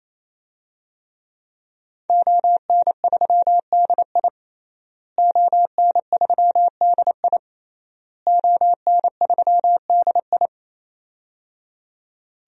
Morse code, heard as a keyed tone, about 28 wpm, 700 Hz. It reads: ON3DI ON3DI ON3DI
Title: DXCC-100 Call Signs - 5 Characters - Repeated 3x - 28wpm